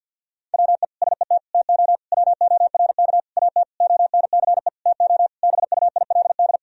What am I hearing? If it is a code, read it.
Morse code, heard as a keyed tone, about 48 wpm, 700 Hz. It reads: WE SET TO WORK AT ONCE TO BUILD
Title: Sentences from Top 1000 Words - Review Part 1 - 50wpm